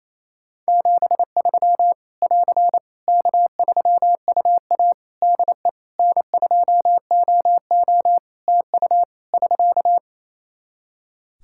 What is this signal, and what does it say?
Morse code, heard as a keyed tone, about 28 wpm, 700 Hz. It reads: 73 <AR> K3UA DE N2OO TU <SK>
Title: K1USN SST QSO Practice 28wpm